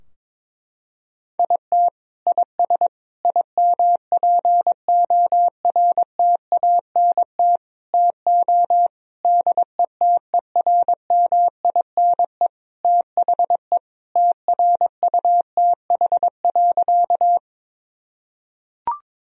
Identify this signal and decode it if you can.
Morse code, heard as a keyed tone, about 22 wpm, 700 Hz. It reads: IT IS IMPORTANT TO DETERMINE THE TRUTH.
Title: Sentences from Top 1000 Words - Review Part 4 - 22wpm